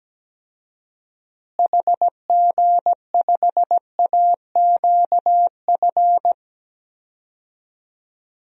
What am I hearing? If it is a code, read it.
Morse code, heard as a keyed tone, about 17 wpm, 700 Hz. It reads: HG5AQF
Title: DXCC-100 Call Signs - 6 Characters - Repeated 3x - 17wpm